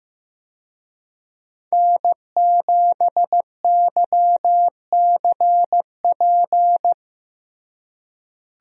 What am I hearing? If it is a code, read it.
Morse code, heard as a keyed tone, about 15 wpm, 700 Hz. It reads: N7YCP
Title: USA Call Signs - 5 Characters - Repeated 2x - 15wpm